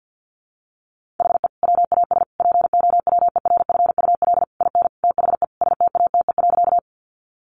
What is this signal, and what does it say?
Morse code, heard as a keyed tone, about 50 wpm, 700 Hz. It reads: HE WAS POWERFUL IN THE STATE.
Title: Sentences from Top 1300 Words 50wpm